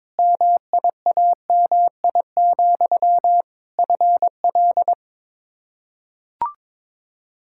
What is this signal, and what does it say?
Morse code, heard as a keyed tone, about 21 wpm, 700 Hz. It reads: MIAMI, FL
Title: K1USN SST QSO Practice 22wpm